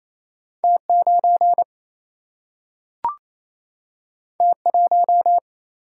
Morse code, heard as a keyed tone, about 28 wpm, 700 Hz. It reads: T9 T1
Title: Single-Letter Single-Number 28wpm